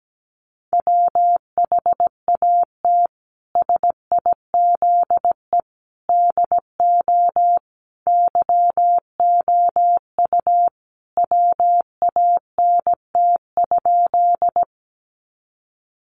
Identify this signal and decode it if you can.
Morse code, heard as a keyed tone, about 17 wpm, 700 Hz. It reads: WHAT SIZE DO YOU WANT?
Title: Sentences from Top 400 Words 17wpm